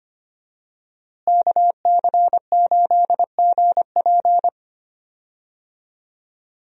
Morse code, heard as a keyed tone, about 25 wpm, 700 Hz. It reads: KC8GP
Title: USA Call Signs - 5 Characters - Repeated 3x - 25wpm